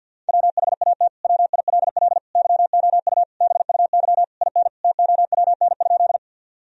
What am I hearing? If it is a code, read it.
Morse code, heard as a keyed tone, about 50 wpm, 700 Hz. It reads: WHAT WILL YOU BUY IN TOWN?
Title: Sentences from Top 1000 Words - Review Part 4 - 50wpm